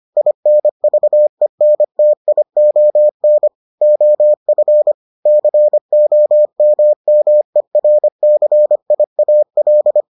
Morse code, heard as a keyed tone, about 25 wpm, 600 Hz. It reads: INVENTION OF COMMERCIAL